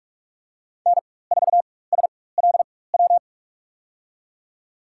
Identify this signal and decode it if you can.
Morse code, heard as a keyed tone, about 44 wpm, spaced as for 21 wpm, 700 Hz. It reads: N4SLW
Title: USA Call Signs - 5 Characters - 4x Character Spacing - 45wpm